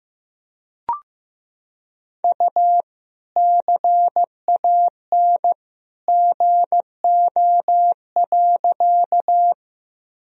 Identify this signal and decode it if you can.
Morse code, heard as a keyed tone, about 15 wpm, 700 Hz. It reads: U CAN GO.
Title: Sentences from Top 100 Words and Abbreviations 15wpm